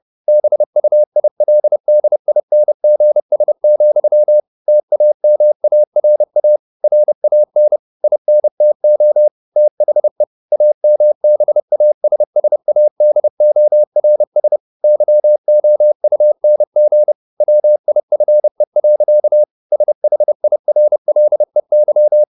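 Morse code, heard as a keyed tone, about 30 wpm, 600 Hz. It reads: BUILDINGS, TAMARA RAN INTO THE AMBASSADORS YOUNG WIFE. SHIRLEY
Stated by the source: Morse code at 29 wpm, sending Ken Follett's novel Never in English, from CW med Gnister.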